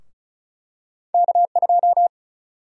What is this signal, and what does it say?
Morse code, heard as a keyed tone, about 36 wpm, 700 Hz. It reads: K2
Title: USA Call Sign Prefixes - Repeated 2x - 35wpm